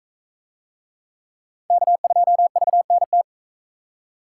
Morse code, heard as a keyed tone, about 42 wpm, 700 Hz. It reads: K2VNT